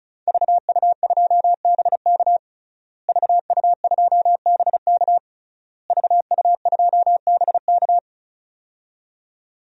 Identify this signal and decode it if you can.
Morse code, heard as a keyed tone, about 34 wpm, 700 Hz. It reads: VU2BK VU2BK VU2BK
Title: DXCC-100 Call Signs - 5 Characters - Repeated 3x - 35wpm